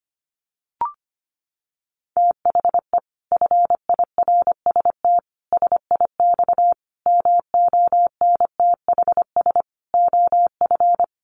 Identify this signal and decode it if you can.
Morse code, heard as a keyed tone, about 25 wpm, 700 Hz. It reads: THE FIRST SIX MONTHS OF
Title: Sets of 5 Words Encore2 25wpm